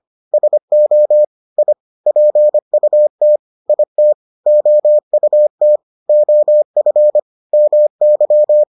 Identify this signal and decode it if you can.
Morse code, heard as a keyed tone, about 25 wpm, 600 Hz. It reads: SO I PUT IT OUT OF MY